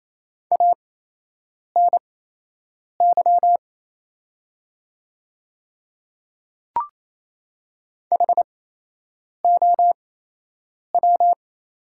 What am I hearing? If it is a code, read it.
Morse code, heard as a keyed tone, about 27 wpm, spaced as for 8 wpm, 700 Hz. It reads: ANY HOW
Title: 8x Character Spacing - Top 200 Words - 28wpm